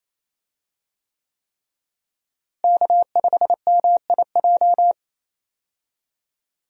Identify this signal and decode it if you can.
Morse code, heard as a keyed tone, about 28 wpm, 700 Hz. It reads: K5MIJ
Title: USA Call Signs - 5 Characters - Repeated 3x - 28wpm